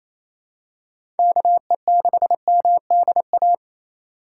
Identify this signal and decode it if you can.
Morse code, heard as a keyed tone, about 28 wpm, 700 Hz. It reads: KE6MDA
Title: USA Call Signs - 2x Character Spacing - 28wpm